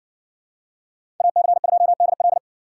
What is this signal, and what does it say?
Morse code, heard as a keyed tone, about 54 wpm, 700 Hz. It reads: AC2DL